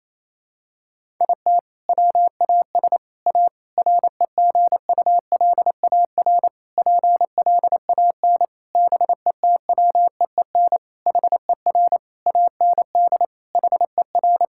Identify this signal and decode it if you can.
Morse code, heard as a keyed tone, about 27 wpm, 700 Hz. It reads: IT WAS A REGULAR PLAN BETWEEN HER AND HER
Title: Sentences from Top 1000 Words - Review Part 6 - 28wpm